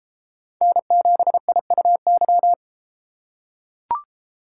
Morse code, heard as a keyed tone, about 33 wpm, 700 Hz. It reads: N7IUY